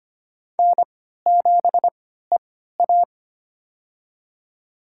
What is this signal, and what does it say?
Morse code, heard as a keyed tone, about 25 wpm, spaced as for 14 wpm, 700 Hz. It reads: N7EA
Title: USA Call Signs - 4 Characters - 3x Character Spacing - 25wpm